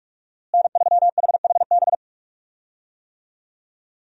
Morse code, heard as a keyed tone, about 44 wpm, 700 Hz. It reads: N3HHB
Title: USA Call Signs - 5 Characters - Repeated 2x - 45wpm